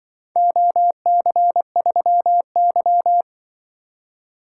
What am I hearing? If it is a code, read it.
Morse code, heard as a keyed tone, about 20 wpm, 700 Hz. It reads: OC3Y
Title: Speed-Racing - Four Letter-Number Combinations - 20wpm